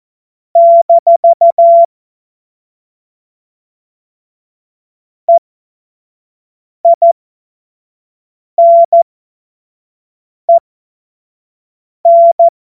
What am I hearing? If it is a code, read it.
Morse code, heard as a keyed tone, about 14 wpm, spaced as for 5 wpm, 680 Hz. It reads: - EINEN
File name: Morsekurs / Witze / Witz 08. November 2022 25 F70